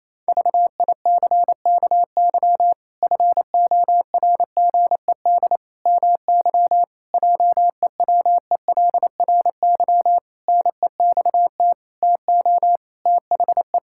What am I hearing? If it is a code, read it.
Morse code, heard as a keyed tone, about 28 wpm, 700 Hz. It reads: VICKY FORGED MY JEWELRY NEXT TO THE
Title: Pangram Challenge v2 - 28wpm